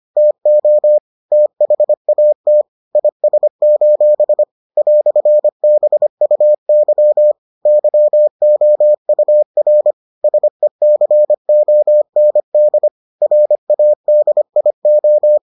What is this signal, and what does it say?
Morse code, heard as a keyed tone, about 25 wpm, 600 Hz. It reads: TO THAT IS: "BUY YOUR SECOND RADIO